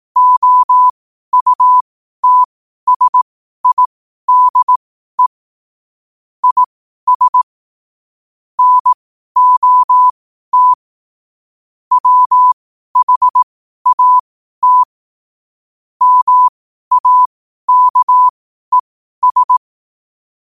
Morse code, heard as a keyed tone, about 18 wpm, spaced as for 12 wpm, 1000 Hz. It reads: OUTSIDE IS NOT WHAT MAKES
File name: qotdfiles / 12wpm